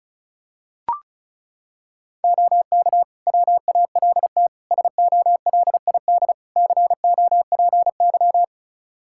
Morse code, heard as a keyed tone, about 36 wpm, 700 Hz. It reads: OK WALT SOLID COPY
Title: Practice QSO - Exchange 3 - More Info - 35wpm